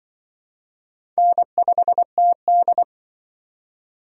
Morse code, heard as a keyed tone, about 25 wpm, 700 Hz. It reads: N5TD